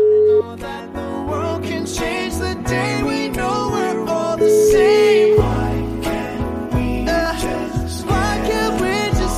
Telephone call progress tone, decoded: ringback tone

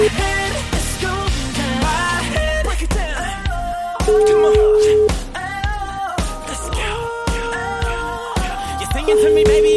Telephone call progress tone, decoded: ringback tone

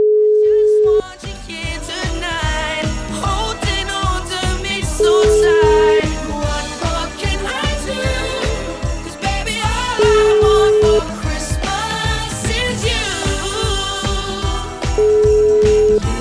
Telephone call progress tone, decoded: ringback tone

